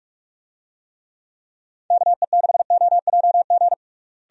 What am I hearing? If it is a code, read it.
Morse code, heard as a keyed tone, about 46 wpm, 700 Hz. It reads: KE6OJG
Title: USA Call Signs - 6 Characters - Repeated 3x - 45wpm